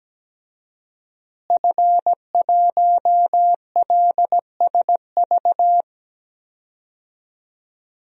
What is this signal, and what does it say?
Morse code, heard as a keyed tone, about 17 wpm, 700 Hz. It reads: F1LSV
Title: DXCC-100 Call Signs - 5 Characters - Repeated 2x - 17wpm